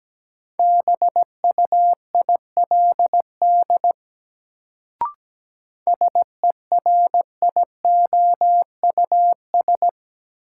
Morse code, heard as a keyed tone, about 17 wpm, 700 Hz. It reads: BUILD SERIOUS
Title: Sentences from Top 1100 Words - New Word Drill Alt - 17wpm